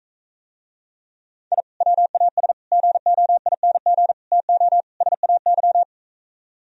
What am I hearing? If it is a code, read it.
Morse code, heard as a keyed tone, about 44 wpm, 700 Hz. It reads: I WAS GOING TO SAY